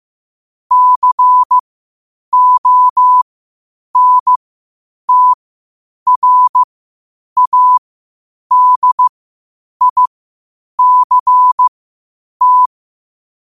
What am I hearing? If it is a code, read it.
Morse code, heard as a keyed tone, about 15 wpm, spaced as for 8 wpm, 1000 Hz. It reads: CONTRADICT
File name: qotdfiles / 8wpm